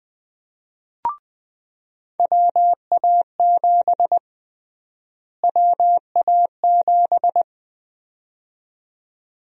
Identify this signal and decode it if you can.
Morse code, heard as a keyed tone, about 20 wpm, 700 Hz. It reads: WA7 WA7
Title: USA Call Sign Prefixes - Repeated 2x - 20wpm